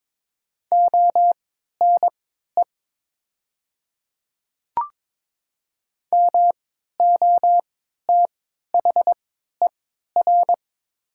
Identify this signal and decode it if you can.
Morse code, heard as a keyed tone, about 22 wpm, spaced as for 12 wpm, 700 Hz. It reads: ONE MOTHER
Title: 3x Character Spacing - Top 200 Words - 22wpm